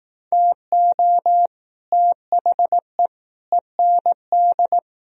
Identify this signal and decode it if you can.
Morse code, heard as a keyed tone, about 18 wpm, 700 Hz. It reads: TO THE END
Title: Speed-Racing - Sets of 4 Words 15wpm